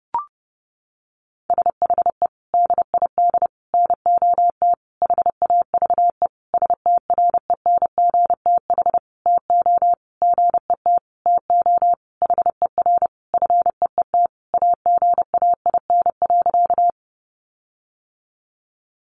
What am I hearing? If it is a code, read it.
Morse code, heard as a keyed tone, about 28 wpm, 700 Hz. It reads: SHE DID NOT HAVE STRENGTH TO GET TO HER FEET AGAIN.
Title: Sentences from Top 1300 Words 30wpm